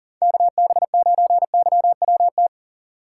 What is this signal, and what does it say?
Morse code, heard as a keyed tone, about 40 wpm, 700 Hz. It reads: KB9YWT
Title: USA Call Signs - 4x Character Spacing - 40wpm